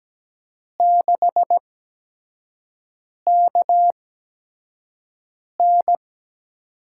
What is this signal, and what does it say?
Morse code, heard as a keyed tone, about 17 wpm, spaced as for 5 wpm, 700 Hz. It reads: 6KN
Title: USA Call Signs - 5 Characters - 8x Character Spacing - 17wpm